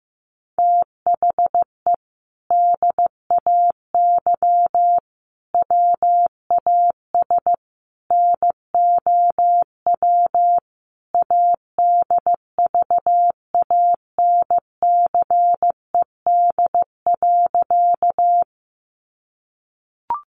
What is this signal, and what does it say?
Morse code, heard as a keyed tone, about 15 wpm, 700 Hz. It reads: THE DAY WAS NOW ADVANCED.
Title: Sentences from Top 1200 Words 15wpm